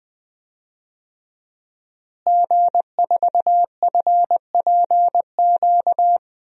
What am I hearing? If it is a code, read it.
Morse code, heard as a keyed tone, about 20 wpm, 700 Hz. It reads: G4FPQ
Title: DXCC-100 Call Signs - 5 Characters - Repeated 3x - 20wpm